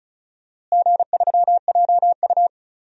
Morse code, heard as a keyed tone, about 34 wpm, 700 Hz. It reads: G3JU
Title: DXCC-100 Call Signs - 4 Characters - Repeated 2x - 35wpm